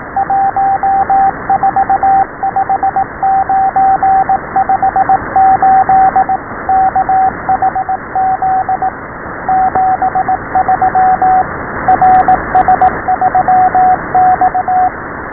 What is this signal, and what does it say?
Morse code, heard as a keyed tone, about 18 wpm, 780 Hz, with radio noise. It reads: 145958 KHZ 73 RS3X
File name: SBFFT12 / RS-12.WAV / RS-12